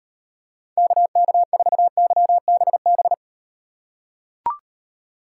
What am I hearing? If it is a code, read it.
Morse code, heard as a keyed tone, about 38 wpm, 700 Hz. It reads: KK4YBB